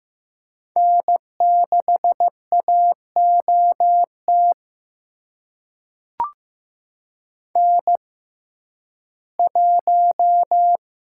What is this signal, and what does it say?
Morse code, heard as a keyed tone, about 15 wpm, spaced as for 11 wpm, 700 Hz. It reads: N6AOT N1
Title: USA Call Signs - 5 Characters - 6x Character Spacing - 15wpm